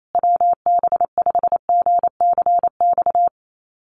Morse code, heard as a keyed tone, about 28 wpm, 700 Hz. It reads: WB5GCX
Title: USA Call Signs Repeated 2x 28wpm